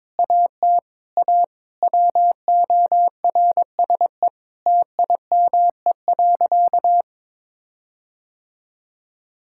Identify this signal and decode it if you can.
Morse code, heard as a keyed tone, about 22 wpm, 700 Hz. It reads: AT A WORSE TIME.
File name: practice / 22wpm / Sentences with 8-Word Phrases - 22wpm